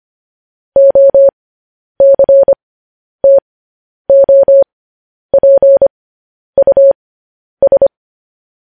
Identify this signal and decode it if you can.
Morse code, heard as a keyed tone, about 25 wpm, spaced as for 10 wpm, 550 Hz. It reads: OCTOPUS
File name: podcast / News headlines-1x-25x10